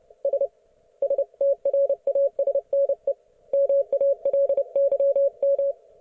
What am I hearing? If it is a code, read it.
Morse code, heard as a keyed tone, about 29 wpm, 560 Hz. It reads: S STRASNE MALYM